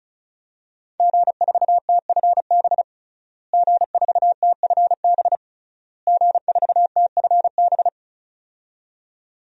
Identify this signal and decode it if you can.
Morse code, heard as a keyed tone, about 34 wpm, 700 Hz. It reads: G4TFB G4TFB G4TFB
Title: DXCC-100 Call Signs - 5 Characters - Repeated 3x - 35wpm